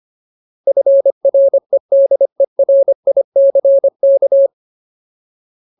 Morse code, heard as a keyed tone, about 25 wpm, 550 Hz. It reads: FREDERICK
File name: Ultimate Word List / 25WPM Ultimate